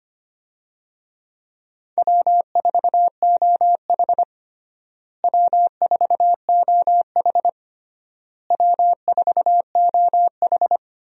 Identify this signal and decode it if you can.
Morse code, heard as a keyed tone, about 25 wpm, 700 Hz. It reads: W4OH W4OH W4OH
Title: USA Call Signs - 4 Characters - Repeated 3x - 25wpm